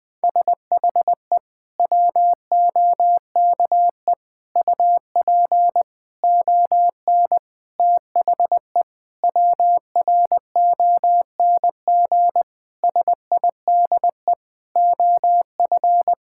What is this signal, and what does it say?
Morse code, heard as a keyed tone, about 20 wpm, 700 Hz. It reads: SHE WOKE UP ON THE WRONG SIDE OF
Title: Sentences with 6-Word Phrases - Spoken Beginning - 20wpm